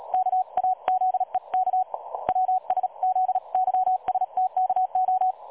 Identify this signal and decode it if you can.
Morse code, heard as a keyed tone, about 36 wpm, 740 Hz. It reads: KAZEK WSZYSTKO
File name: dlazaawansowanych / nagraniazpasma / websdr recording 3565.0kHz